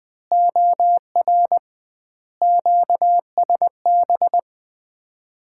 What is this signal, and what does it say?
Morse code, heard as a keyed tone, about 21 wpm, 700 Hz. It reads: OR QSB
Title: Common QSO Phrases - 2x Word-Spacing - 20wpm